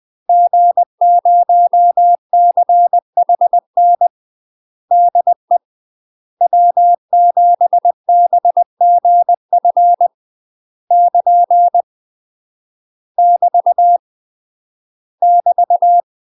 Morse code, heard as a keyed tone, about 20 wpm, 700 Hz. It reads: G0CHN DE W7BGF <KN> <BT> <BT>
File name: UK MASTER QSOs TRAINING / 20WPM / UK MASTER QSOs 0060